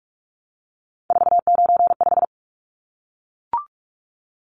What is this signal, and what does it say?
Morse code, heard as a keyed tone, about 44 wpm, 700 Hz. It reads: 495
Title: Numbers - 3-Digit 45wpm